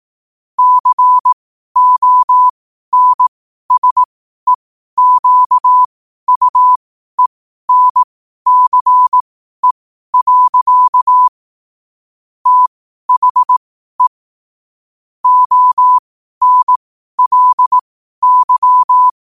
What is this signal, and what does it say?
Morse code, heard as a keyed tone, about 18 wpm, spaced as for 12 wpm, 1000 Hz. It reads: CONSEQUENCE. THE ONLY
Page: New quotes every day in morse code at 12 Words per minute.